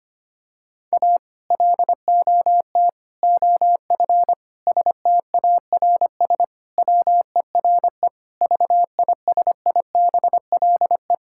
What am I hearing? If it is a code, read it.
Morse code, heard as a keyed tone, about 25 wpm, 700 Hz. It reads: A LOT OF STARS WERE VISIBLE
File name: practice / 25wpm / Sentences with 2-Word Phrases - 25wpm